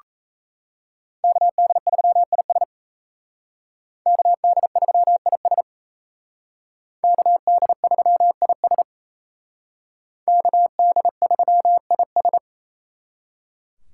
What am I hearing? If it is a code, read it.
Morse code, heard as a keyed tone, about 35 wpm, 700 Hz. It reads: KD3IS KD3IS KD3IS KD3IS